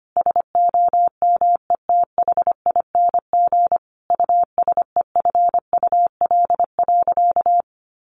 Morse code, heard as a keyed tone, about 25 wpm, 700 Hz. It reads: SOMETHING USEFUL.
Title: Sentences from Top 1400 Words 25wpm